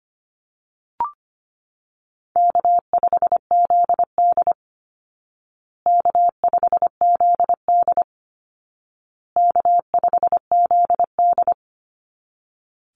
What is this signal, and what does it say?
Morse code, heard as a keyed tone, about 25 wpm, 700 Hz. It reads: K5ZD K5ZD K5ZD
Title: Easy Call Signs Repeated 3x - 25wpm